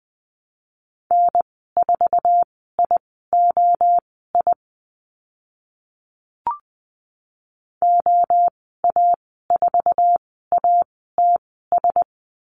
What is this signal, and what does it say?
Morse code, heard as a keyed tone, about 20 wpm, spaced as for 14 wpm, 700 Hz. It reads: N4IOI OA4ATS
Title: 062 Lesson 11g - Letter h - Call Signs 20wpm